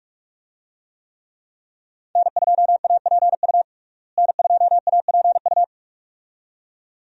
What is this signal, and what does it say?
Morse code, heard as a keyed tone, about 44 wpm, 700 Hz. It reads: N2APU N2APU
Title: USA Call Signs - 5 Characters - Repeated 2x - 45wpm